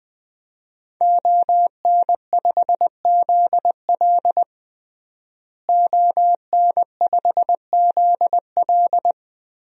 Morse code, heard as a keyed tone, about 20 wpm, 700 Hz. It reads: ON5ZL ON5ZL
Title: DXCC-100 Call Signs - 5 Characters - Repeated 2x - 20wpm